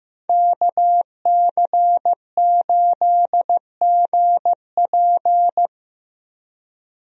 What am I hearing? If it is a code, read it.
Morse code, heard as a keyed tone, about 15 wpm, 700 Hz. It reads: KC8GP
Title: USA Call Signs - 5 Characters - Repeated 3x - 15wpm